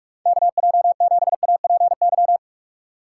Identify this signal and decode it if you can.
Morse code, heard as a keyed tone, about 46 wpm, 700 Hz. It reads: KJ7APY
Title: USA Call Signs - 6 Characters - 3x Character Spacing - 45wpm